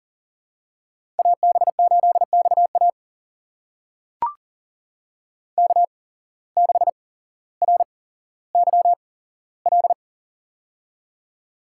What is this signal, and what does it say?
Morse code, heard as a keyed tone, about 40 wpm, spaced as for 17 wpm, 700 Hz. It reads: AB8XA K6RYL
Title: USA Call Signs - 5 Characters - 8x Character Spacing - 40wpm